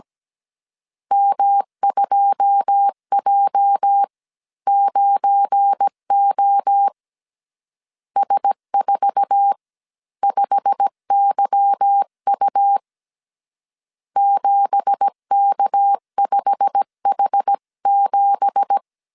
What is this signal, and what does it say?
Morse code, heard as a keyed tone, about 17 wpm, spaced as for 14 wpm, 800 Hz. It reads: M2J9O S45YU 7K5H7
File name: CW / DET / 1 9 1500C S C 16wpm 2909